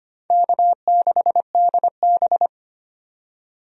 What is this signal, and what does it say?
Morse code, heard as a keyed tone, about 25 wpm, 700 Hz. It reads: K6DB
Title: USA Call Signs - 4 Characters - 25wpm